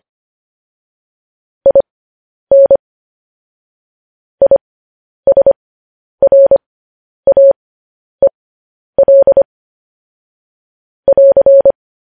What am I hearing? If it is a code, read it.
Morse code, heard as a keyed tone, about 25 wpm, spaced as for 10 wpm, 550 Hz. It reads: IN ISRAEL <AR>